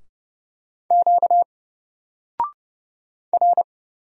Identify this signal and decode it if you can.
Morse code, heard as a keyed tone, about 30 wpm, 700 Hz. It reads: Q R
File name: Single-Letter-Number-30wpm